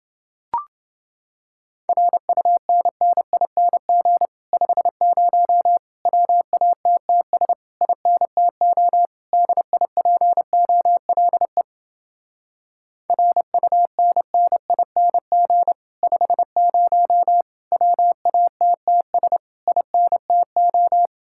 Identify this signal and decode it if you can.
Morse code, heard as a keyed tone, about 29 wpm, 700 Hz. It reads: RUNNING 50 WATTS INTO DIPOLE RUNNING 50 WATTS INTO